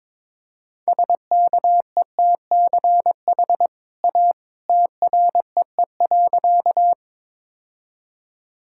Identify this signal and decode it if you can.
Morse code, heard as a keyed tone, about 21 wpm, 700 Hz. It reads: SKETCH A TREE.
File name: practice / 22wpm / Easy Sentences with Hints - 22wpm